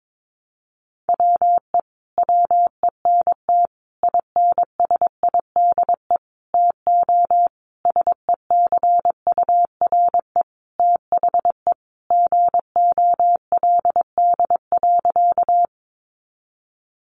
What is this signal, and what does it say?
Morse code, heard as a keyed tone, about 22 wpm, 700 Hz. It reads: WE WENT INSIDE TO SECURE THE GOLD.
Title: Sentences from Top 1200 Words 22wpm